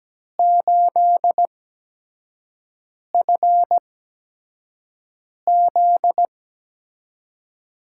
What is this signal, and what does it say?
Morse code, heard as a keyed tone, about 17 wpm, spaced as for 5 wpm, 700 Hz. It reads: 8FZ
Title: USA Call Signs - 6 Characters - 8x Character Spacing - 17wpm